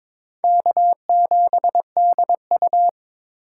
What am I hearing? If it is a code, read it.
Morse code, heard as a keyed tone, about 22 wpm, 700 Hz. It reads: K7DU
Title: USA Call Signs - 8x Character Spacing - 22wpm